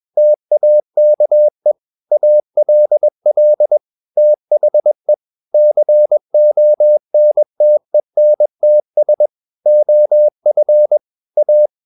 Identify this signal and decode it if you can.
Morse code, heard as a keyed tone, about 21 wpm, 600 Hz. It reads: TAKE ALL THE CONTENTS OF A
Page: CW med Gnister